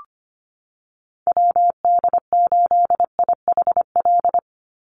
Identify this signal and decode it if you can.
Morse code, heard as a keyed tone, about 25 wpm, 700 Hz. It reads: WD8IHL
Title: USA Call Signs 25wpm